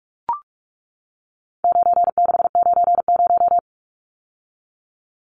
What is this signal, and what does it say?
Morse code, heard as a keyed tone, about 44 wpm, 700 Hz. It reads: 9690
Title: Numbers - 4-Digit 45wpm